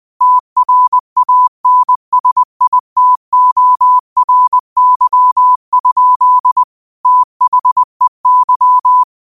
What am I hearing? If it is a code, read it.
Morse code, heard as a keyed tone, about 20 wpm, 1000 Hz. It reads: TRANSITORY? THEY
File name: 20wpm